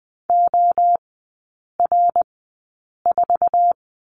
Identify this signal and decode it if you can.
Morse code, heard as a keyed tone, about 20 wpm, spaced as for 8 wpm, 700 Hz. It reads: OR4
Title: 053 Lesson 10e - Letter r - Three Characters 20wpm